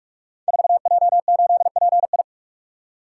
Morse code, heard as a keyed tone, about 44 wpm, 700 Hz. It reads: 4J8PI
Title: Code Groups - 5 Letter:Numbers - 45wpm